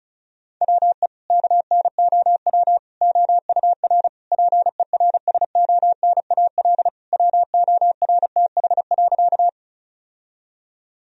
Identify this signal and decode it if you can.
Morse code, heard as a keyed tone, about 36 wpm, 700 Hz. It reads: WE KNOW OUR PERSONAL WORTH.
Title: Sentences from Top 1000 Words - Review Part 4 - 35wpm